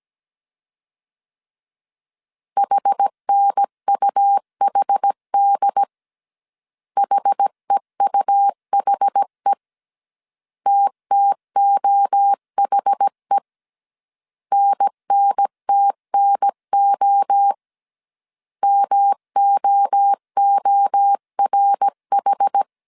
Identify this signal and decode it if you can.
Morse code, heard as a keyed tone, about 17 wpm, 800 Hz. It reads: HNUHD HEUHE TTOHE NNTNO MOORH
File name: CW / DET / 1 3 1500C S A 16wpm 2141